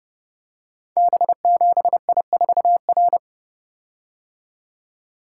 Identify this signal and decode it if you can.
Morse code, heard as a keyed tone, about 30 wpm, 700 Hz. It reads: B7I4R
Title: Code Groups - 5 Letter:Numbers - 30wpm